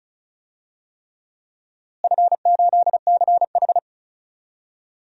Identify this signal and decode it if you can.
Morse code, heard as a keyed tone, about 36 wpm, 700 Hz. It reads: F8CH
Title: DXCC-100 Call Signs - 4 Characters - Repeated 3x - 35wpm